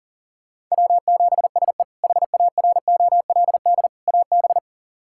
Morse code, heard as a keyed tone, about 40 wpm, 700 Hz. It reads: W7SE HAROLD AB
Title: K1USN SST QSO Practice - 40wpm